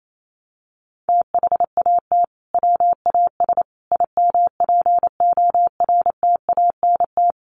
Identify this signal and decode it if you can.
Morse code, heard as a keyed tone, about 28 wpm, 700 Hz. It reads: THAT WAS IMPORTANT
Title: Sentences from Top 600 Words 28wpm